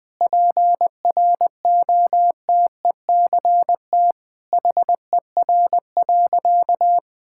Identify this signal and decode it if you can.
Morse code, heard as a keyed tone, about 20 wpm, 700 Hz. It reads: PROTECT HER.
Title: Sentences from Top 1600 Words 20wpm